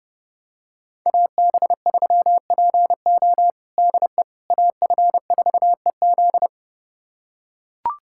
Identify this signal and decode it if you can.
Morse code, heard as a keyed tone, about 30 wpm, 700 Hz. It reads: AB3PO DE AF4EZ
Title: USA Callsign DE Callsign - 5 Characters - 30wpm